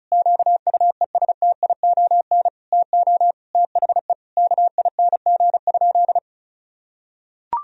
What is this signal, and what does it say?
Morse code, heard as a keyed tone, about 36 wpm, 700 Hz. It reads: QUESTION TO THE KING?
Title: Sentences from Top 1000 Words - Review Part 2 - 35wpm